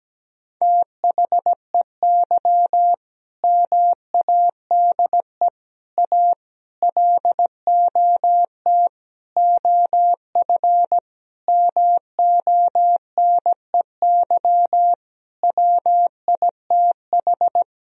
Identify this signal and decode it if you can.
Morse code, heard as a keyed tone, about 17 wpm, 700 Hz. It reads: THEY MADE A LOT OF MONEY WITH
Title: Sentences from Top 1000 Words - Review Part 4 - 17wpm